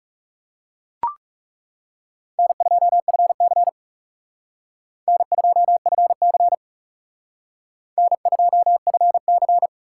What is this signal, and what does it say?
Morse code, heard as a keyed tone, about 38 wpm, 700 Hz. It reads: N2FC N2FC N2FC